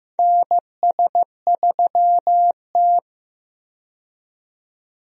Morse code, heard as a keyed tone, about 16 wpm, 700 Hz. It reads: NS3T
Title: USA Call Signs - 4 Characters - 15wpm